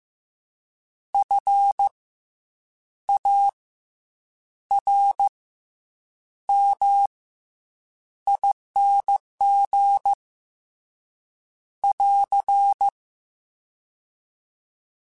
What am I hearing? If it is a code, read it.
Morse code, heard as a keyed tone, about 15 wpm, 800 Hz. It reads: F A R M ING <AR>
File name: Special INGs / Special INGs-mp3s / SlowToFaster ing @13